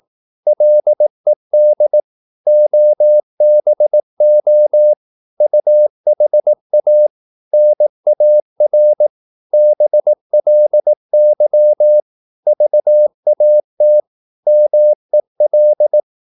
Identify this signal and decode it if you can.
Morse code, heard as a keyed tone, about 18 wpm, 600 Hz. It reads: LED OBO UHA NAR BLY VAT MEL
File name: Korte ord DK 18wpm